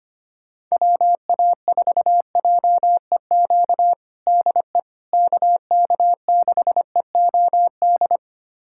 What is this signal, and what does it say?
Morse code, heard as a keyed tone, about 25 wpm, 700 Hz. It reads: WA4JEQ DE KK6EOD
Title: USA Callsign DE Callsign 25wpm